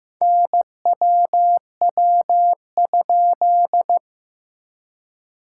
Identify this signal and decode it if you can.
Morse code, heard as a keyed tone, about 15 wpm, 700 Hz. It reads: NWW?
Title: SOTA QSO Practice 15wpm